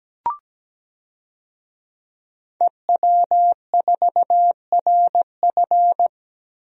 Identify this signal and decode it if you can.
Morse code, heard as a keyed tone, about 17 wpm, 700 Hz. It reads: EW4RF
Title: DXCC-100 Call Signs - 5 Characters - 17wpm